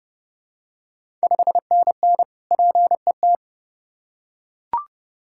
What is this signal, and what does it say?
Morse code, heard as a keyed tone, about 31 wpm, 700 Hz. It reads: 5NN PET
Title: Ontario QSO Party Exchange 30wpm